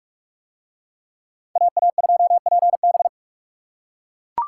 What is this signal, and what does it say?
Morse code, heard as a keyed tone, about 39 wpm, 700 Hz. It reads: AA2PB